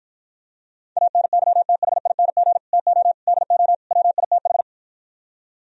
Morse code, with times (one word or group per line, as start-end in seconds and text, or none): 1.0-2.6 ANYTHING
2.7-3.1 TO
3.3-3.7 DO
3.9-4.6 WITH